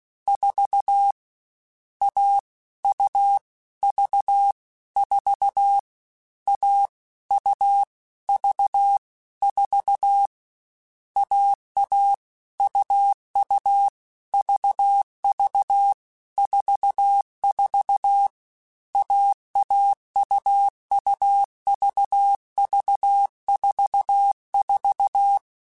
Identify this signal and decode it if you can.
Morse code, heard as a keyed tone, about 16 wpm, 800 Hz. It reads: 4 A U V 4 A U V 4 AA UU VV 44 AAUUVV44